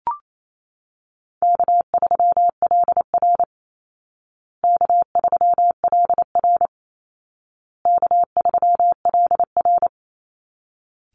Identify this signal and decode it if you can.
Morse code, heard as a keyed tone, about 28 wpm, 700 Hz. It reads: K3LR K3LR K3LR
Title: Call Signs Repeated 2x - 28wpm